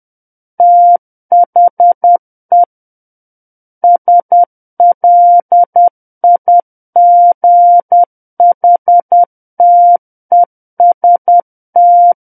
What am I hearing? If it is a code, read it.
Morse code, heard as a keyed tone, about 10 wpm, 700 Hz. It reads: THE SLIGHTEST
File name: Book One / 01-Chapter 1 - 10 WPM